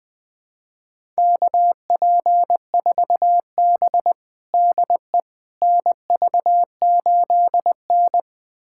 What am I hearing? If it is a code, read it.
Morse code, heard as a keyed tone, about 20 wpm, 700 Hz. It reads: KP4B DE NV8N
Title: USA Callsign DE Callsign - 4 Characters - 20wpm